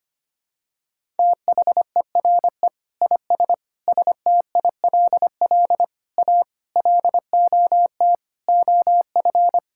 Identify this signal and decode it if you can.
Morse code, heard as a keyed tone, about 25 wpm, 700 Hz. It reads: THERE IS STILL A LOT OF
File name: Sets of 9 Words - 25wpm